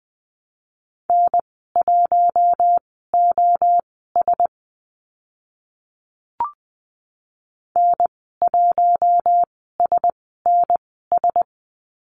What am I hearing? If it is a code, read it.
Morse code, heard as a keyed tone, about 20 wpm, spaced as for 14 wpm, 700 Hz. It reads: N1OS N1SNS
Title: 042 Lesson 8g - Number 1 - Call Signs 20wpm